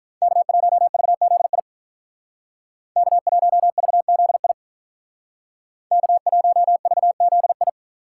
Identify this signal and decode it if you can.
Morse code, heard as a keyed tone, about 46 wpm, 700 Hz. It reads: K1VZI K1VZI K1VZI